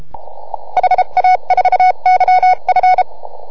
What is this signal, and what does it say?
Morse code, heard as a keyed tone, about 33 wpm, spaced as for 29 wpm, 1480 Hz. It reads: HA4YF